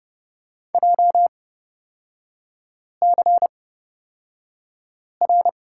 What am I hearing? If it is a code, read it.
Morse code, heard as a keyed tone, about 31 wpm, spaced as for 5 wpm, 700 Hz. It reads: JCR
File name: Lesson 38e - Letter q - Three Characters 30wpm